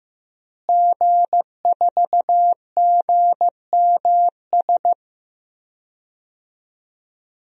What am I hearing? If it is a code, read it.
Morse code, heard as a keyed tone, about 15 wpm, 700 Hz. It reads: G4GMS
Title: DXCC-100 Call Signs - 5 Characters - Repeated 3x - 15wpm